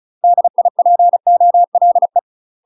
Morse code, heard as a keyed tone, about 34 wpm, 700 Hz. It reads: DIPOLE
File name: UK MASTER QSOs TRAINING / 35WPM / UK MASTER QSOs 0001